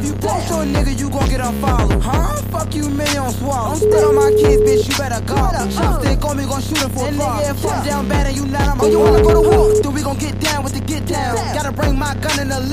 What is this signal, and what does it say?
Ringback tone (Europe, 425 Hz), with noise.